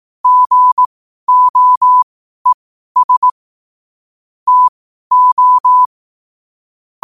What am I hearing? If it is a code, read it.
Morse code, heard as a keyed tone, about 18 wpm, spaced as for 12 wpm, 1000 Hz. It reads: GOES TO
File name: qotdfiles / 12wpm